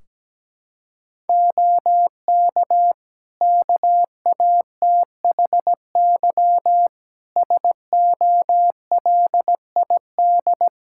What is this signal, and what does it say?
Morse code, heard as a keyed tone, about 17 wpm, 700 Hz. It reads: OK KATHY SOLID
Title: Practice QSO - Exchange 3 - More Info - 17wpm